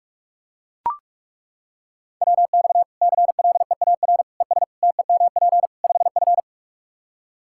Morse code, heard as a keyed tone, about 42 wpm, 700 Hz. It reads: WX CLEAR ES TEMP 5F
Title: Practice QSO - Exchange 2 - Info - 45wpm